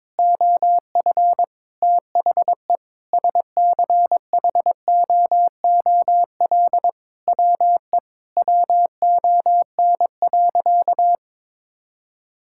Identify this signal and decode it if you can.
Morse code, heard as a keyed tone, about 22 wpm, 700 Hz. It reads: OF THE SCHOOL WE WON.
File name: Sentences with 9-Word Phrases - 22wpm